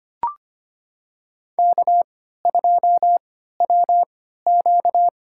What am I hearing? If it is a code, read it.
Morse code, heard as a keyed tone, about 25 wpm, spaced as for 14 wpm, 700 Hz. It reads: K2WQ
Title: USA Call Signs - 4 Characters - 3x Character Spacing - 25wpm